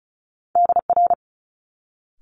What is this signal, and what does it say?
Morse code, heard as a keyed tone, about 36 wpm, 700 Hz. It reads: DR
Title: Multi-letter Phonemes Quick Fire 35wpm